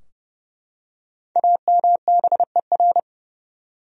Morse code, heard as a keyed tone, about 30 wpm, 700 Hz. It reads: AMBER
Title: USA Common Names - Max 5 Letters - 30wpm